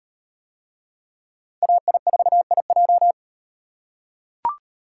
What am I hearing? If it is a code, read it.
Morse code, heard as a keyed tone, about 37 wpm, 700 Hz. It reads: AI4IJ